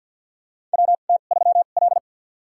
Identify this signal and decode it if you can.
Morse code, heard as a keyed tone, about 49 wpm, spaced as for 36 wpm, 700 Hz. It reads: WT3L
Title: USA Call Signs - 2x Character Spacing - 50wpm